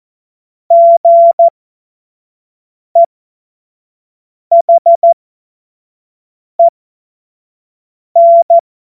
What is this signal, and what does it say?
Morse code, heard as a keyed tone, about 14 wpm, spaced as for 5 wpm, 680 Hz. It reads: GEHEN
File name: Witz-09-September-2025-25-F70